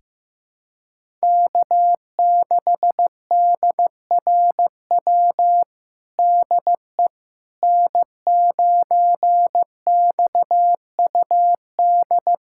Morse code, heard as a keyed tone, about 15 wpm, 700 Hz. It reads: K6DRW DE N9XUD
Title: USA Callsign DE Callsign - 5 Characters - 15wpm